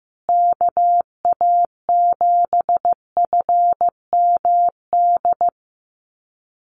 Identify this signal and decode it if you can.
Morse code, heard as a keyed tone, about 15 wpm, 700 Hz. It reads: KA7FMD
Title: USA Call Signs Repeated 2x 15wpm